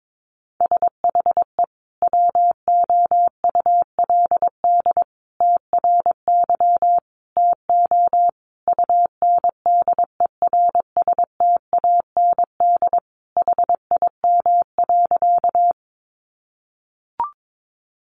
Morse code, heard as a keyed tone, about 22 wpm, 700 Hz. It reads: SHE WOULD TRY TO UNDERSTAND HIM.
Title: Sentences from Top 600 Words 22wpm